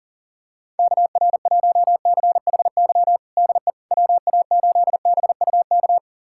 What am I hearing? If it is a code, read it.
Morse code, heard as a keyed tone, about 40 wpm, 700 Hz. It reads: KR1CHY DE WA8BUK
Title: USA Callsign DE Callsign - 6 Characters - 40wpm